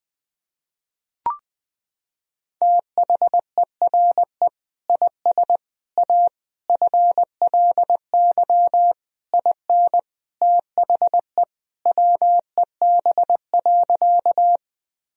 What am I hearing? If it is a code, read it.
Morse code, heard as a keyed tone, about 20 wpm, 700 Hz. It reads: THERE IS A FLY IN THE WEB.
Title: Sentences from Top 1000 Words - Review Part 4 - 20wpm